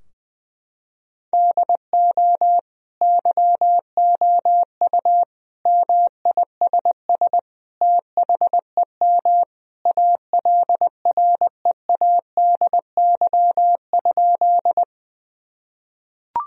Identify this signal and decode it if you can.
Morse code, heard as a keyed tone, about 20 wpm, 700 Hz. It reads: DO YOU MISS THEM ALREADY?
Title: Sentences from Top 1000 Words - Review Part 3 - 20wpm